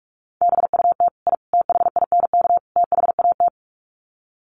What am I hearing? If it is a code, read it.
Morse code, heard as a keyed tone, about 44 wpm, 700 Hz. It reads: BUT I THINK THAT
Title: Sets of 4 Words 45wpm